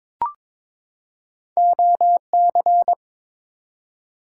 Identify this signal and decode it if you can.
Morse code, heard as a keyed tone, about 22 wpm, 700 Hz. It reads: OC
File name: practice / 22wpm / Two Letter-Number Combinations - 22wpm